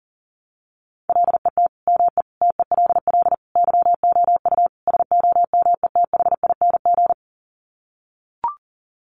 Morse code, heard as a keyed tone, about 37 wpm, 700 Hz. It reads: LET ME TELL YOU SOMETHING
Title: Sets of 5 Words 40wpm